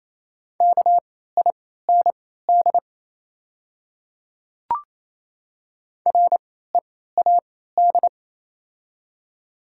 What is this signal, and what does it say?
Morse code, heard as a keyed tone, about 28 wpm, spaced as for 16 wpm, 700 Hz. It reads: KIND READ
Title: 3x Character Spacing - Top 200 Words - 28wpm